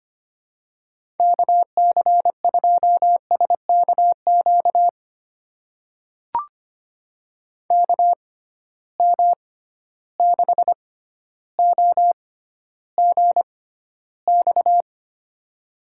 Morse code, heard as a keyed tone, about 25 wpm, spaced as for 13 wpm, 700 Hz. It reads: KC2SKQ KM6OGX
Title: USA Call Signs - 6 Characters - 6x Character Spacing - 25wpm